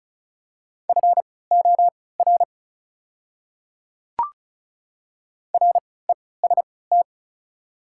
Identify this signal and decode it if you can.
Morse code, heard as a keyed tone, about 34 wpm, spaced as for 20 wpm, 700 Hz. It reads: FOR REST
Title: 3x Character Spacing - Top 500 Words - 35wpm